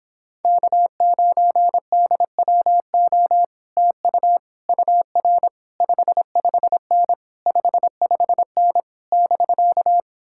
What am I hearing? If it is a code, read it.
Morse code, heard as a keyed tone, about 26 wpm, 700 Hz. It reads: K9DWO TU UR 55N 55N <BK>